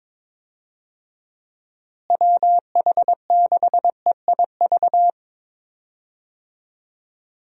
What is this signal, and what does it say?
Morse code, heard as a keyed tone, about 22 wpm, 700 Hz. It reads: WH6EIV
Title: USA Call Signs - 6 Characters - Repeated 3x - 22wpm